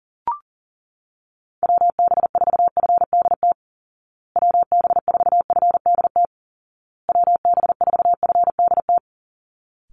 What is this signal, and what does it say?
Morse code, heard as a keyed tone, about 40 wpm, 700 Hz. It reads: WB4FDT WB4FDT WB4FDT
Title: Call Signs Repeated 2x - 40wpm